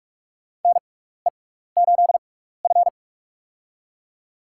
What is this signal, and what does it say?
Morse code, heard as a keyed tone, about 44 wpm, spaced as for 15 wpm, 700 Hz. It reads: NE8F
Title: USA Call Signs - 4 Characters - 6x Character Spacing - 45wpm